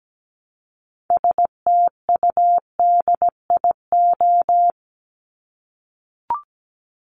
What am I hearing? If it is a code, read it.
Morse code, heard as a keyed tone, about 17 wpm, 700 Hz. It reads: STUDIO
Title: Top 1001-1250 Words 17wpm